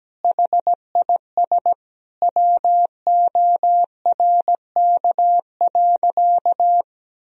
Morse code, heard as a keyed tone, about 17 wpm, 700 Hz. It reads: HIS WORK.
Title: Sentences from Top 1000 Words - Review Part 4 - 17wpm